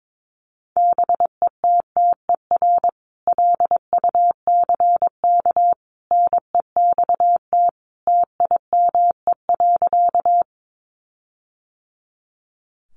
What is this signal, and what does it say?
Morse code, heard as a keyed tone, about 22 wpm, 700 Hz. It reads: BETTER LUCK NEXT TIME.
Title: Sentences from Top 1500 Words 22wpm